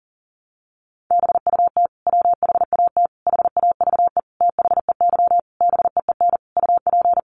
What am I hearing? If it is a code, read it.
Morse code, heard as a keyed tone, about 40 wpm, 700 Hz. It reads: BUT WHAT HAVE THEY BEEN UP
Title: Sentences from Top 100 Words 40wpm